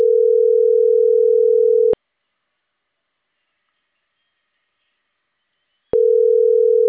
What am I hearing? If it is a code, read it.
Ringback tone (North America), 440 + 480 Hz.